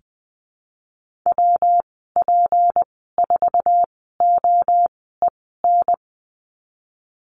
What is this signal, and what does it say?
Morse code, heard as a keyed tone, about 20 wpm, spaced as for 15 wpm, 700 Hz. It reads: WP4OEN
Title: 161 Lesson 26g - Letter g - Call Signs 20wpm